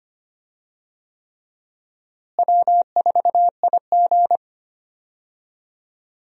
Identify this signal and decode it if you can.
Morse code, heard as a keyed tone, about 25 wpm, 700 Hz. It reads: W4IG
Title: USA Call Signs - 4 Characters - Repeated 3x - 25wpm